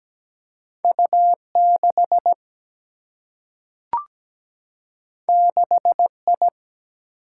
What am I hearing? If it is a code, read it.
Morse code, heard as a keyed tone, about 17 wpm, 700 Hz. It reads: U 6 6 I
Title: Single-Letter Single-Number 17wpm